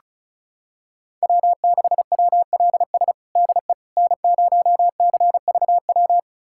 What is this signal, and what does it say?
Morse code, heard as a keyed tone, about 36 wpm, 700 Hz. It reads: W6WLS DE N0CVW
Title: USA Callsign DE Callsign - 5 Characters - 35wpm